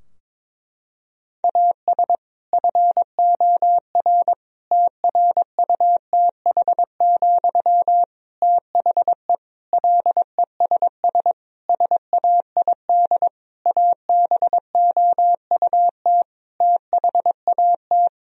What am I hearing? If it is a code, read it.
Morse code, heard as a keyed tone, about 22 wpm, 700 Hz. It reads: AS FOR TRUTH, THE LESS SAID ABOUT THAT
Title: Sentences from Top 1000 Words - Review Part 2 - 22wpm